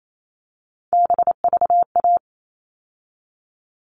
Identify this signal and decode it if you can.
Morse code, heard as a keyed tone, about 28 wpm, 700 Hz. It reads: BVA
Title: All Cut Numbers - 3-Digit 28wpm